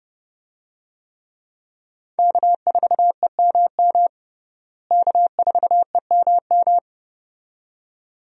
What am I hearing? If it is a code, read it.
Morse code, heard as a keyed tone, about 32 wpm, 700 Hz. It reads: K4EMM K4EMM
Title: USA Call Signs - 5 Characters - Repeated 2x - 30wpm